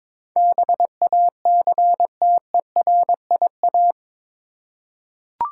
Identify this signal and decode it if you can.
Morse code, heard as a keyed tone, about 22 wpm, 700 Hz. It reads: BACTERIA
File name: practice / 22wpm / Eight Letter Words - 22wpm